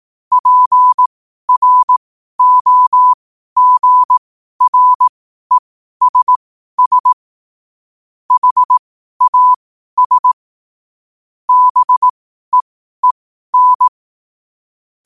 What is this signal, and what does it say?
Morse code, heard as a keyed tone, about 18 wpm, spaced as for 12 wpm, 1000 Hz. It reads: PROGRESS HAS BEEN
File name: qotdfiles / 12wpm